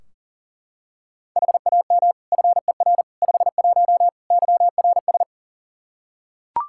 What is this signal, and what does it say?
Morse code, heard as a keyed tone, about 37 wpm, 700 Hz. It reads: HAM FER 51 YRS
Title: Practice QSO - Exchange 3 - More Info - 40wpm